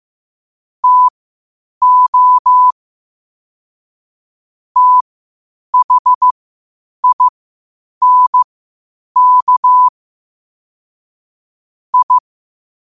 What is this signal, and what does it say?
Morse code, heard as a keyed tone, about 15 wpm, spaced as for 8 wpm, 1000 Hz. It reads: TO THINK I
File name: qotdfiles / 8wpm